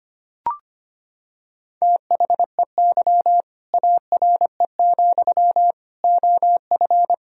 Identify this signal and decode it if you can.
Morse code, heard as a keyed tone, about 25 wpm, 700 Hz. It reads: THEY ARE, OF
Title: Sentences from Top 1000 Words - Review Part 4 - 25wpm